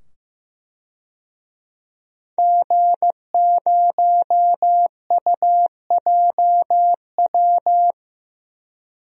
Morse code, heard as a keyed tone, about 15 wpm, 700 Hz. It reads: G0UJW
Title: DXCC-100 Call Signs - 5 Characters - 15wpm